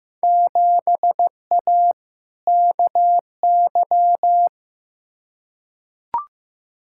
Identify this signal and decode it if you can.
Morse code, heard as a keyed tone, about 16 wpm, 700 Hz. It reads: 7A KY
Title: Field Day Exchanges Encore - 15wpm